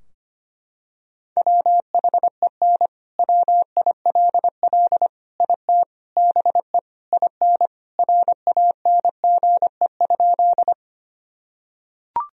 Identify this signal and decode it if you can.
Morse code, heard as a keyed tone, about 25 wpm, 700 Hz. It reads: WHEN WILL IT BE IN RANGE?
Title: Sentences from Top 1000 Words - Review Part 4 - 25wpm